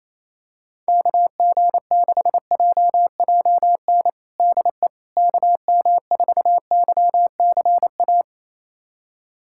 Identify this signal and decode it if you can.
Morse code, heard as a keyed tone, about 27 wpm, 700 Hz. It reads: KG6JJN DE KM4YCA
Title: USA Callsign DE Callsign - 6 Characters - 28wpm